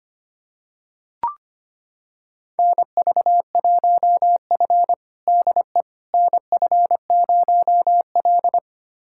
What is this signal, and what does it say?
Morse code, heard as a keyed tone, about 25 wpm, 700 Hz. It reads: NV1F DE NF0L
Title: USA Callsign DE Callsign - 4 Characters - 25wpm